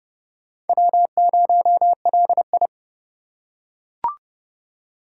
Morse code, heard as a keyed tone, about 30 wpm, 700 Hz. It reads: W0LI
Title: USA Call Signs - 4 Characters - 2x Character Spacing - 30wpm